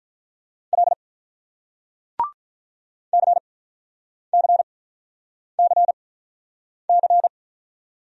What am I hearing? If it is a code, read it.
Morse code, heard as a keyed tone, about 48 wpm, spaced as for 9 wpm, 700 Hz. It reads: L CCCC